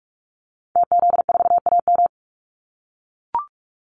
Morse code, heard as a keyed tone, about 44 wpm, 700 Hz. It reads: TZ4AM
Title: Call Signs - 45wpm